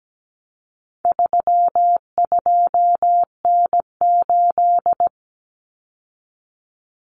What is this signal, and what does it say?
Morse code, heard as a keyed tone, about 17 wpm, 700 Hz. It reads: 32N8
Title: Common Cut Numbers - 4-Digit 17wpm